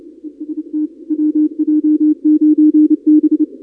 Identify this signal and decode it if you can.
Morse code, heard as a keyed tone, about 28 wpm, 310 Hz. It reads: EST WJ9B
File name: contests / 2019-cw-wpx-kt5j-k5pi- -2330 0-31